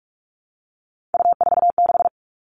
Morse code, heard as a keyed tone, about 46 wpm, 700 Hz. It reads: U46
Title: Three Challenging Letter-Number Combinations 45wpm